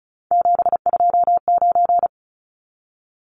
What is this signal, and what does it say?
Morse code, heard as a keyed tone, about 34 wpm, 700 Hz. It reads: 729
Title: Numbers - 3-Digit 35wpm